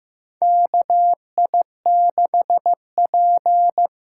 Morse code, heard as a keyed tone, about 15 wpm, 700 Hz. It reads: KI6P
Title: USA Call Signs - 6 Characters - 8x Character Spacing - 15wpm